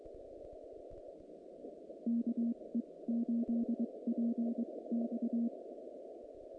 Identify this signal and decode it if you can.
Morse code, heard as a keyed tone, about 22 wpm, spaced as for 18 wpm, 230 Hz. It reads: KE8PX
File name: contests / naqp-cw-k5tr-op-k5pi- -2200